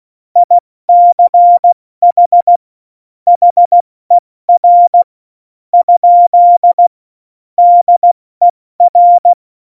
Morse code, heard as a keyed tone, about 16 wpm, spaced as for 13 wpm, 700 Hz. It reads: ICH HER? DER
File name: Morsekurs / Witze / Witz 19. Februar 2019, 70 F80